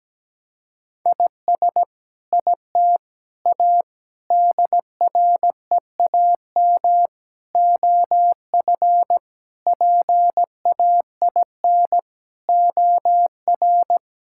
Morse code, heard as a keyed tone, about 17 wpm, 700 Hz. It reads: IS IT A DREAM OF PAIN OR
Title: Sentences from Top 1600 Words 17wpm